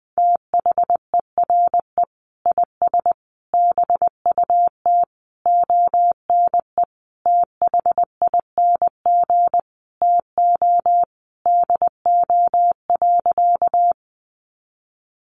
Morse code, heard as a keyed tone, about 19 wpm, 700 Hz. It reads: THERE IS BUT ONE THING TO DO.
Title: Sentences from Top 100 Words 20wpm